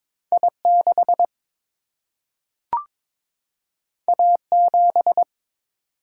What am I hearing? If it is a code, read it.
Morse code, heard as a keyed tone, about 22 wpm, 700 Hz. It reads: I 6 A 7
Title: Single-Letter Single-Number 22wpm